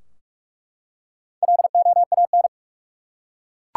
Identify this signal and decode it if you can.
Morse code, heard as a keyed tone, about 43 wpm, 700 Hz. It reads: LOAN
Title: Four Letter Words - 45wpm